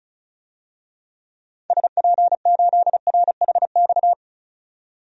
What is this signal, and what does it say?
Morse code, heard as a keyed tone, about 36 wpm, 700 Hz. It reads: SP8RHX
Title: DXCC-100 Call Signs - 6 Characters - 35wpm